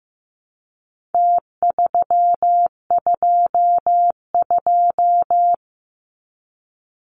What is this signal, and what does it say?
Morse code, heard as a keyed tone, about 15 wpm, 700 Hz. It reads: T322
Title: Numbers - Contest Style - 4-Digit 15wpm